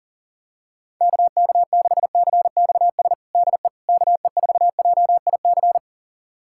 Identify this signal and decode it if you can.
Morse code, heard as a keyed tone, about 40 wpm, 700 Hz. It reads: KK6CXS DE KE4JIC
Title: USA Callsign DE Callsign - 6 Characters - 40wpm